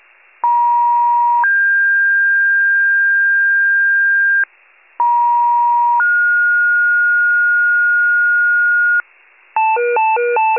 Two-tone page, heard as a one-second tone, then a three-second tone, about 950 then 1600 Hz, about 950 then 1430 Hz, with noise. It is heard from 0.4 to 9.0 s.